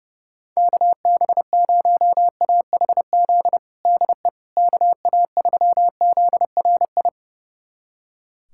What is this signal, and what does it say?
Morse code, heard as a keyed tone, about 30 wpm, 700 Hz. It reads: KB0AHZ DE KA3ZRI
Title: USA Callsign DE Callsign - 6 Characters - 30wpm